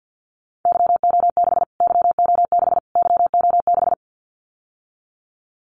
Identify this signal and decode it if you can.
Morse code, heard as a keyed tone, about 50 wpm, 700 Hz. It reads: YO6 YO6 YO6
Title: Callsign Prefixes for Top 25 DXCC - Repeated 3x - 50wpm